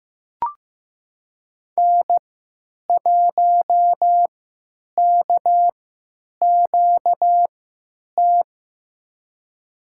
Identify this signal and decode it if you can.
Morse code, heard as a keyed tone, about 15 wpm, spaced as for 9 wpm, 700 Hz. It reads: N1KQT
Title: USA Call Signs - 5 Characters - 3x Character Spacing - 15wpm